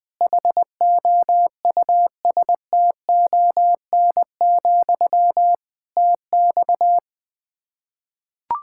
Morse code, heard as a keyed tone, about 20 wpm, 700 Hz. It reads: HOUSTON, TX
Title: K1USN SST QSO Practice 20wpm